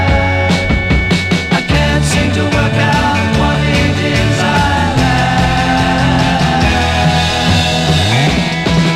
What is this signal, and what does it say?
Dial-up modem handshake.